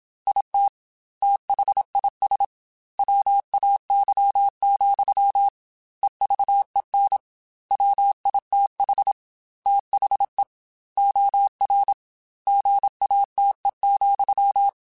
Morse code, heard as a keyed tone, about 27 wpm, 800 Hz. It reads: IT THIS WAY, EVEN WITH THE OR GATE,
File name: Technical Subjects / Technical Subjects-mp3s / OR Gate @25